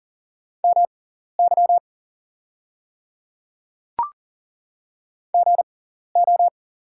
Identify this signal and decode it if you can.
Morse code, heard as a keyed tone, about 40 wpm, spaced as for 14 wpm, 690 Hz. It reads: MY GO
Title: 6x Character Spacing - Top 500 Words - 40wpm